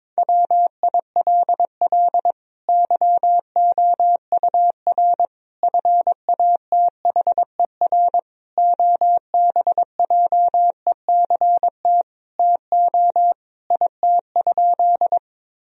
Morse code, heard as a keyed tone, about 22 wpm, 700 Hz. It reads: WILL YOUR FATHER OBJECT TO IT?
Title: Sentences from Top 1000 Words - Review Part 3 - 22wpm